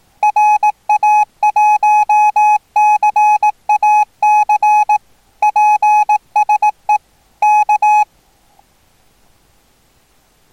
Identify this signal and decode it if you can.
Morse code, heard as a keyed tone, about 18 wpm, 840 Hz. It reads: RA1CAC PSE K
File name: attachments / 1cq noagc 2cq agcin 3cq agcout